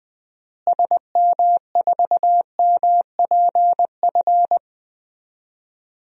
Morse code, heard as a keyed tone, about 20 wpm, 700 Hz. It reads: SM4MPF
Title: DXCC-100 Call Signs - 6 Characters - Repeated 2x - 20wpm